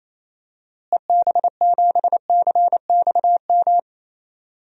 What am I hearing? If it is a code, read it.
Morse code, heard as a keyed tone, about 28 wpm, 700 Hz. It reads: EB7CXM
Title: DXCC-100 Call Signs - 6 Characters - Repeated 2x - 28wpm